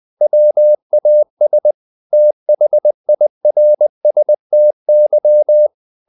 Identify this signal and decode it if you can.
Morse code, heard as a keyed tone, about 20 wpm, 600 Hz. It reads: WAS THIRSTY
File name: the old man and the sea / the old man and the sea part 1 20wpm